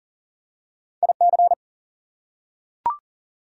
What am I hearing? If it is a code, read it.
Morse code, heard as a keyed tone, about 40 wpm, 700 Hz. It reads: IC
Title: Prefixes and Suffixes 40wpm